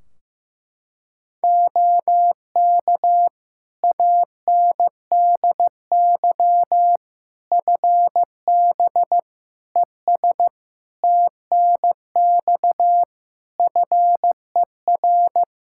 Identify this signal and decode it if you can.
Morse code, heard as a keyed tone, about 15 wpm, 700 Hz. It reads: OK ANDY FB ES TNX FER
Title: K1USN SST QSO Practice 15wpm